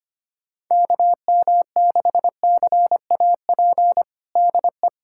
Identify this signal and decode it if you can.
Morse code, heard as a keyed tone, about 25 wpm, 700 Hz. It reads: KM6CAP DE
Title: USA Callsign DE Callsign 25wpm